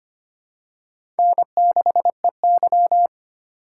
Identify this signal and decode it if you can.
Morse code, heard as a keyed tone, about 25 wpm, 700 Hz. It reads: N6EY
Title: USA Call Signs - 4 Characters - 4x Character Spacing - 25wpm